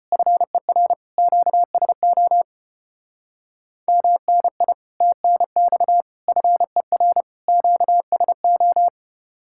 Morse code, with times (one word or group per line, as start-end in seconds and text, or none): 0.1-0.9 FER
1.2-2.4 QSO
3.9-4.7 MNI
5.0-6.0 TNX
6.3-7.2 FER
7.5-8.9 QSO